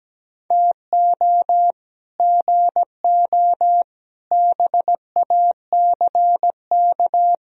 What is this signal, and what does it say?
Morse code, heard as a keyed tone, about 17 wpm, 700 Hz. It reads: TO GO BACK
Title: Speed-Racing - Sets of 5 Words 17wpm